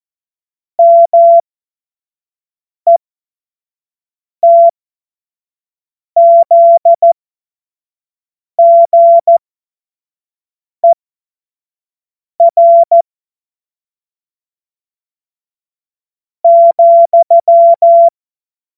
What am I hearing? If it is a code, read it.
Morse code, heard as a keyed tone, about 14 wpm, spaced as for 4 wpm, 680 Hz. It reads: METZGER,